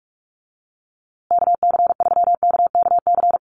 Morse code, heard as a keyed tone, about 44 wpm, 690 Hz. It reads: KC3KKC
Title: USA Call Signs Repeated 3x 45wpm